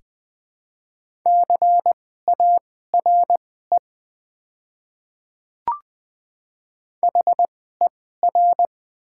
Morse code, heard as a keyed tone, about 20 wpm, spaced as for 14 wpm, 700 Hz. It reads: CARE HER
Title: 2x Character Spacing - Top 500 Words - 20wpm